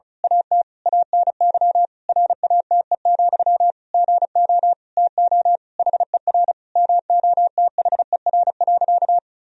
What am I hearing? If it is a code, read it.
Morse code, heard as a keyed tone, about 34 wpm, 700 Hz. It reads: AT ANY RATE, GO TO HER MOTHER.
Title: Sentences from Top 1000 Words - Review Part 4 - 35wpm